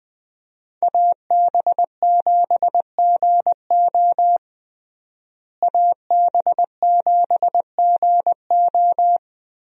Morse code, heard as a keyed tone, about 20 wpm, 700 Hz. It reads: AB7GO AB7GO
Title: USA Call Signs - 5 Characters - Repeated 2x - 20wpm